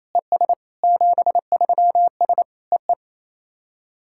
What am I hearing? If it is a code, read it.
Morse code, heard as a keyed tone, about 28 wpm, 700 Hz. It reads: ES 73S EE
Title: SOTA QSO Practice 28wpm